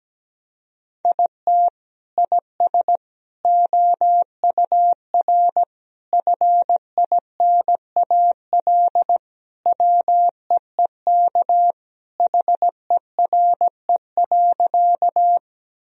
Morse code, heard as a keyed tone, about 17 wpm, 700 Hz. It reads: IT IS OUR FINAL WEEK HERE.
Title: Sentences from Top 1000 Words - Review Part 2 - 17wpm